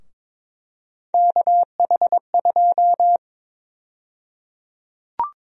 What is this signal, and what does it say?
Morse code, heard as a keyed tone, about 22 wpm, 700 Hz. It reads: KH2
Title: USA Call Sign Prefixes - 22wpm